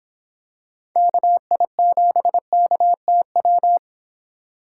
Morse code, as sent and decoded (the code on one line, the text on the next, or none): -.- .. --... -.- - .--
KI7KTW